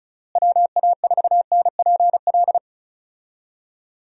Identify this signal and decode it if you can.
Morse code, heard as a keyed tone, about 36 wpm, 700 Hz. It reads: WA4NPL
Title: USA Call Signs - 6 Characters - Repeated 3x - 35wpm